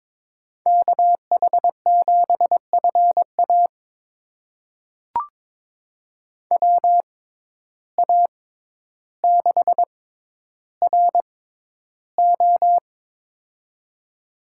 Morse code, heard as a keyed tone, about 23 wpm, spaced as for 11 wpm, 700 Hz. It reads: KH7FA WA6RO
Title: USA Call Signs - 5 Characters - 6x Character Spacing - 22wpm